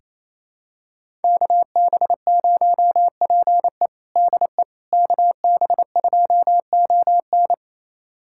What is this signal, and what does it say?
Morse code, heard as a keyed tone, about 28 wpm, 700 Hz. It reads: KB0PE DE KB2ON
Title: USA Callsign DE Callsign - 5 Characters - 28wpm